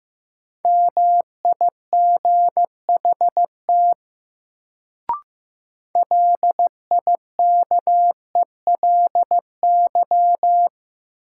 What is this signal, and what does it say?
Morse code, heard as a keyed tone, about 15 wpm, 700 Hz. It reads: MIGHT LIKELY
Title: Sets of 2 Words - New Word Drill - 15wpm